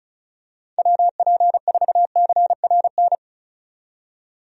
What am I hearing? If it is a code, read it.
Morse code, heard as a keyed tone, about 34 wpm, 700 Hz. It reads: WP4CRN
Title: USA Call Signs - 6 Characters - 35wpm